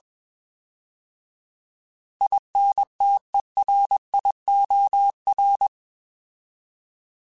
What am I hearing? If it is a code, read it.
Morse code, heard as a keyed tone, about 21 wpm, 800 Hz. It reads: INTERIOR